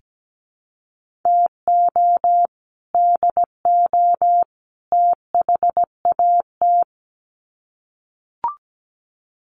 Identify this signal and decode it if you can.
Morse code, heard as a keyed tone, about 17 wpm, 700 Hz. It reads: TO DO THAT
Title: Sets of 3 Words 17wpm